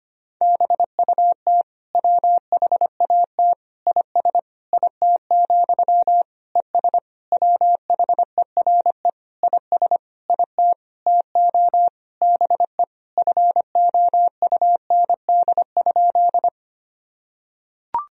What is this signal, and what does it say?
Morse code, heard as a keyed tone, about 25 wpm, 700 Hz. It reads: BUT WHAT IS IT, ES WHERE IS IT TO BE FOUND?
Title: Sentences from Top 300 Words and Abbreviations 25wpm